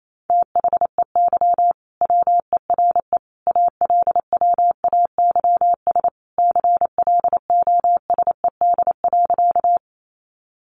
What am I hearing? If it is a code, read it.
Morse code, heard as a keyed tone, about 28 wpm, 700 Hz. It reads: THEY WERE ALWAYS CLOSED.
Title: Sentences from Top 1400 Words 28wpm